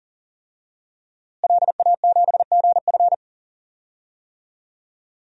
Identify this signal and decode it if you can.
Morse code, heard as a keyed tone, about 40 wpm, 700 Hz. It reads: LA7GF
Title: DXCC-100 Call Signs - 5 Characters - Repeated 3x - 40wpm